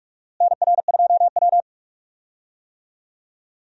Morse code, heard as a keyed tone, about 43 wpm, 700 Hz. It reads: NR2W
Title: USA Call Signs - 4 Characters - Repeated 3x - 45wpm